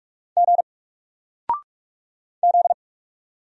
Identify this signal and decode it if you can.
Morse code, heard as a keyed tone, about 44 wpm, 700 Hz. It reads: G Z
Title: Single Character (Letter:Number:Punctuation:Prosign) 45wpm